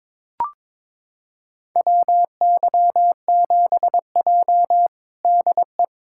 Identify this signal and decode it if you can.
Morse code, heard as a keyed tone, about 23 wpm, 700 Hz. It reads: WY7J DE
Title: USA Callsign DE Callsign - 4 Characters - 22wpm